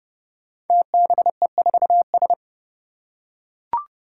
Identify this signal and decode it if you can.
Morse code, heard as a keyed tone, about 30 wpm, 700 Hz. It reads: TBE4S
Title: Five Challenging Letter-Number Combinations 30wpm